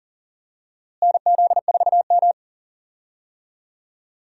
Morse code, heard as a keyed tone, about 38 wpm, 700 Hz. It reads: NZ4M
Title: USA Call Signs - 4 Characters - Repeated 2x - 40wpm